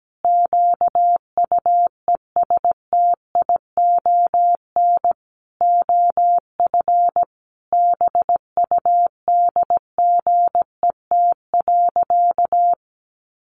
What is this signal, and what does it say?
Morse code, heard as a keyed tone, about 17 wpm, 700 Hz. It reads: QUESTION OF BUDGET.
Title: Sentences from Top 1200 Words 17wpm